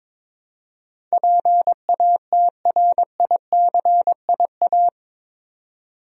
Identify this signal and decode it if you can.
Morse code, heard as a keyed tone, about 22 wpm, 700 Hz. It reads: PATRICIA
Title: USA Common Names - 22wpm